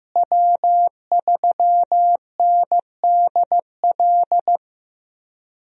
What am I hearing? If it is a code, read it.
Morse code, heard as a keyed tone, about 15 wpm, 700 Hz. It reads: W3NDL
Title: USA Call Signs - 5 Characters - Repeated 3x - 15wpm